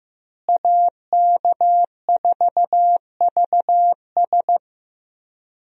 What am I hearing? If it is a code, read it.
Morse code, heard as a keyed tone, about 15 wpm, 700 Hz. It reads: AK4VS
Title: USA Call Signs - 5 Characters - Repeated 3x - 15wpm